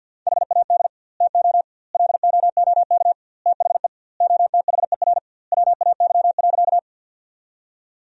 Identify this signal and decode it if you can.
Morse code, heard as a keyed tone, about 50 wpm, 700 Hz. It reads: HAD TO LOOK THE OTHER WAY.
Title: 2x Word-Spacing - Sentences from Top 100 Words - 50wpm